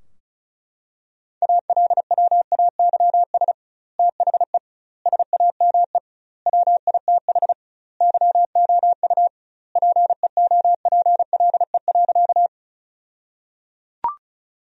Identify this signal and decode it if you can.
Morse code, heard as a keyed tone, about 36 wpm, 700 Hz. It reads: ALWAYS THE SAME WITH YOU PEOPLE.
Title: 2x Word-Spacing - Sentences from Top 300 Words - 35wpm